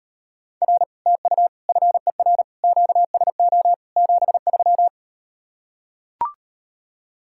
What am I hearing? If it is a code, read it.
Morse code, heard as a keyed tone, about 38 wpm, 700 Hz. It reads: R TU FER QSO 73